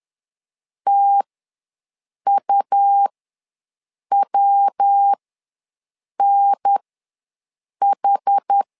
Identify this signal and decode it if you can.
Morse code, heard as a keyed tone, about 10 wpm, spaced as for 6 wpm, 800 Hz. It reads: TUWNH
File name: LEZIONI / 3 MWRO / 3 MW ws 10wpm 2213